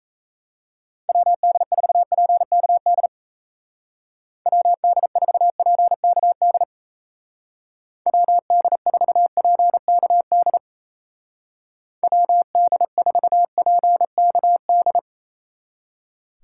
Morse code, heard as a keyed tone, about 34 wpm, 700 Hz. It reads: WD4PKD WD4PKD WD4PKD WD4PKD
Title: Speed Racing - USA Call Signs 28wpm